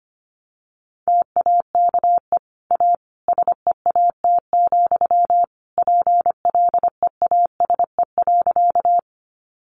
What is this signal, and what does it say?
Morse code, heard as a keyed tone, about 25 wpm, 700 Hz. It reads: TAKE A SEAT, PLEASE.
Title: Sentences from Top 1500 Words 25wpm